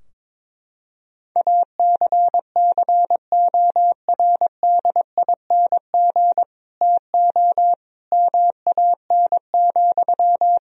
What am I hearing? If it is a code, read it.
Morse code, heard as a keyed tone, about 22 wpm, 700 Hz. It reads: ACCORDING TO MAN,
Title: Sentences from Top 1000 Words - Review Part 4 - 22wpm